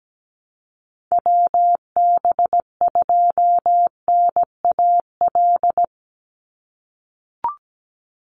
Morse code, heard as a keyed tone, about 17 wpm, 700 Hz. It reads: WB2NAL
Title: USA Call Signs 17wpm